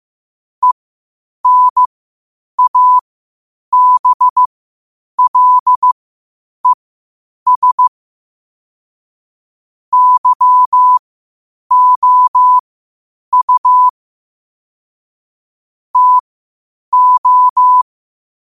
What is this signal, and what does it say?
Morse code, heard as a keyed tone, about 15 wpm, spaced as for 8 wpm, 1000 Hz. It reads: ENABLES YOU TO
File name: qotdfiles / 8wpm